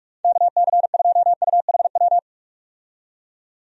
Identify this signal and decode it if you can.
Morse code, heard as a keyed tone, about 43 wpm, 700 Hz. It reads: KC2UHW
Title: USA Call Signs - 6 Characters - 45wpm